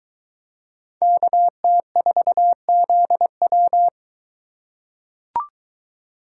Morse code, heard as a keyed tone, about 22 wpm, 700 Hz. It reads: KT4ZW